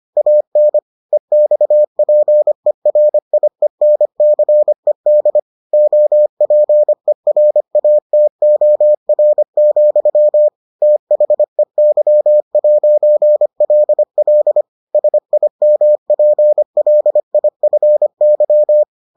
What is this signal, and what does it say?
Morse code, heard as a keyed tone, about 25 wpm, 600 Hz. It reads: AN EXPERIENCED OPERATOR, THEY'LL SIMPLIFY